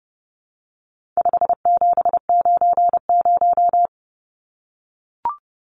Morse code, heard as a keyed tone, about 30 wpm, 700 Hz. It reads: 5790
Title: Numbers - 4-Digit 30wpm